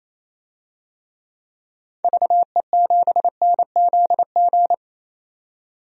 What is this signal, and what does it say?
Morse code, heard as a keyed tone, about 28 wpm, 700 Hz. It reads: VE7NZG
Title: DXCC-100 Call Signs - 6 Characters - 28wpm